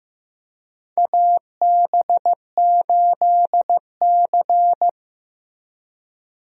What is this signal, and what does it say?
Morse code, heard as a keyed tone, about 15 wpm, 700 Hz. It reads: AB8C
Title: USA Call Signs - 4 Characters - Repeated 2x - 15wpm